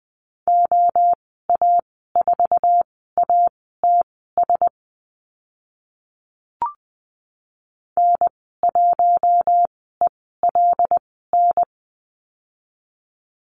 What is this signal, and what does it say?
Morse code, heard as a keyed tone, about 20 wpm, spaced as for 14 wpm, 700 Hz. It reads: OA4ATS N1ELN
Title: 076 Lesson 13g - Letter L - Call Signs 20wpm